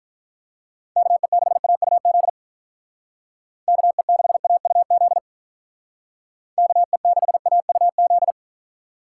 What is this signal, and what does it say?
Morse code, heard as a keyed tone, about 44 wpm, 700 Hz. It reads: KE6AUZ KE6AUZ KE6AUZ
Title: Speed Racing - USA 2x3 Call Signs 35wpm